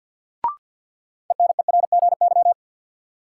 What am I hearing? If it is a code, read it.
Morse code, heard as a keyed tone, about 46 wpm, 700 Hz. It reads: ENERGY
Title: Sentences from Top 900 Words - New Word Drill - 50wpm